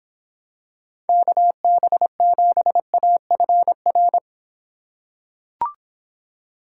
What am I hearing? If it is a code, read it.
Morse code, heard as a keyed tone, about 25 wpm, 700 Hz. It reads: KB7AFR